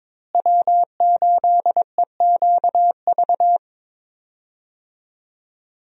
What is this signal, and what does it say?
Morse code, heard as a keyed tone, about 22 wpm, 700 Hz. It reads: W8EQV
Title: USA Call Signs - 5 Characters - Repeated 3x - 22wpm